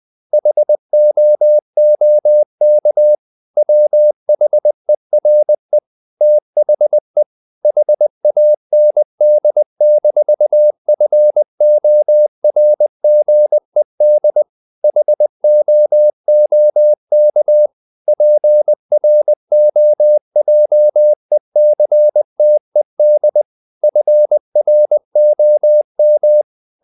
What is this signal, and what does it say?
Morse code, heard as a keyed tone, about 20 wpm, 600 Hz. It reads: HOOK WHERE THE HAND-FORGED HOOK PROJECTED FROM
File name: the old man and the sea / the old man and the sea part 1 20wpm